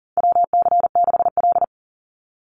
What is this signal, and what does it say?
Morse code, heard as a keyed tone, about 40 wpm, 700 Hz. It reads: WC6L
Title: USA Call Signs 40wpm